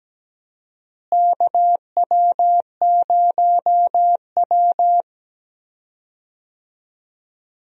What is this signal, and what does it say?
Morse code, heard as a keyed tone, about 17 wpm, 700 Hz. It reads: KW0W
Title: USA Call Signs - 4 Characters - Repeated 3x - 17wpm